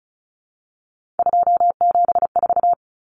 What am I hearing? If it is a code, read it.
Morse code, heard as a keyed tone, about 34 wpm, 700 Hz. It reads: 274
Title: Numbers - 3-Digit 35wpm